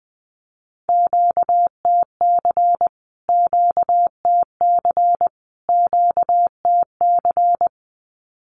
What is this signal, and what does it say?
Morse code, heard as a keyed tone, about 20 wpm, 700 Hz. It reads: QTC QTC QTC
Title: Q-Codes Repeated 3x 20wpm